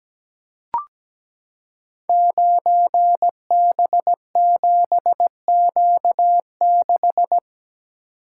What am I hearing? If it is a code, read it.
Morse code, heard as a keyed tone, about 17 wpm, 700 Hz. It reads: 9B7Q6
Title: Code Groups - 5 Letter:Numbers - 17wpm